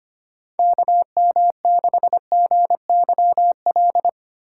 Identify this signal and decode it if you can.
Morse code, heard as a keyed tone, about 25 wpm, 700 Hz. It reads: KM6GYL
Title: USA Call Signs - 6 Characters - 2x Character Spacing - 25wpm